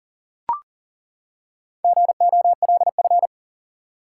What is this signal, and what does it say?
Morse code, heard as a keyed tone, about 40 wpm, 700 Hz. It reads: GOLF
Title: Four Letter Words - 40wpm